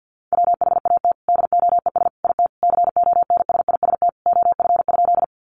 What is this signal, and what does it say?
Morse code, heard as a keyed tone, about 50 wpm, 700 Hz. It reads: WHAT DOES IT CONSIST OF?
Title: Sentences from Top 1400 Words 50wpm